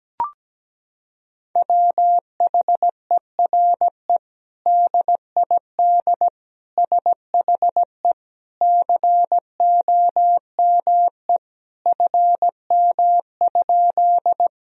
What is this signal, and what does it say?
Morse code, heard as a keyed tone, about 17 wpm, 700 Hz. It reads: WHERE DID SHE COME FM?
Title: Sentences from Top 200 Words and Abbreviations 17wpm